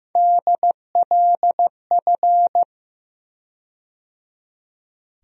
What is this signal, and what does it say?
Morse code, heard as a keyed tone, about 15 wpm, 700 Hz. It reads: DLF
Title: USA Call Sign Postfixes - 3 Letters 15wpm